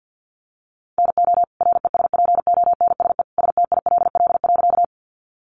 Morse code, heard as a keyed tone, about 50 wpm, 700 Hz. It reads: NO RESPONSE STILL.
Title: Sentences from Top 1400 Words 50wpm